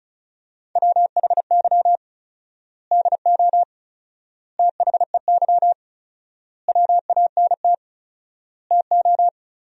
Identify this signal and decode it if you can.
Morse code, heard as a keyed tone, about 36 wpm, 700 Hz. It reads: WHY DO THEY WANT TO
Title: 4x Word-Spacing - Sentences from Top 400 Words - 35wpm